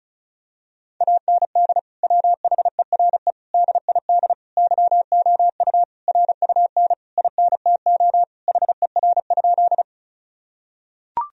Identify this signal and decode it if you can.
Morse code, heard as a keyed tone, about 34 wpm, 700 Hz. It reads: AND WHERE DID YOU RUN INTO HER?
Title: Sentences from Top 1000 Words - Review Part 1 - 35wpm